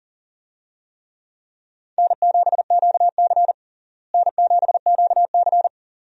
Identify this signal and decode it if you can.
Morse code, heard as a keyed tone, about 40 wpm, 700 Hz. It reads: N7QC N7QC
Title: USA Call Signs - 4 Characters - Repeated 2x - 40wpm